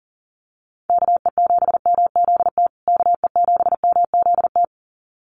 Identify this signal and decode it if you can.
Morse code, heard as a keyed tone, about 40 wpm, 700 Hz. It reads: KE7MZT KE7MZT
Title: USA Call Signs Repeated 2x 40wpm